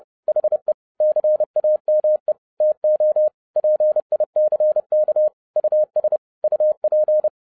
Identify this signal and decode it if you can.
Morse code, heard as a keyed tone, about 30 wpm, 610 Hz. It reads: HE CAME TO PICK US UP